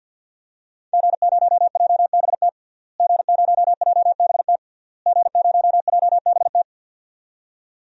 Morse code, heard as a keyed tone, about 48 wpm, 700 Hz. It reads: G0JBT G0JBT G0JBT
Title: DXCC-100 Call Signs - 5 Characters - Repeated 3x - 50wpm